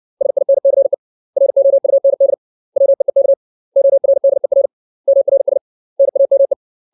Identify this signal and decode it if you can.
Morse code, heard as a keyed tone, about 60 wpm, 550 Hz. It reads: SINCE ROUND WEEK ORDER MRS NAME
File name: 632 words / 632 Words 60WPM